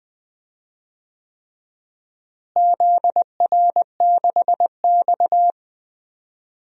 Morse code, heard as a keyed tone, about 20 wpm, 700 Hz. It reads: ZR6X
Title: DXCC-100 Call Signs - 4 Characters - Repeated 3x - 20wpm